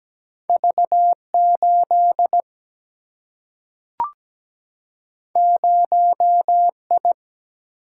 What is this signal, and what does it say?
Morse code, heard as a keyed tone, about 17 wpm, 700 Hz. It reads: V8 0I
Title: Single-Letter Single-Number 17wpm